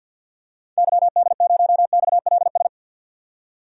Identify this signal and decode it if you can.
Morse code, heard as a keyed tone, about 52 wpm, 700 Hz. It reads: YD0XLS
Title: DXCC-100 Call Signs - 6 Characters - 50wpm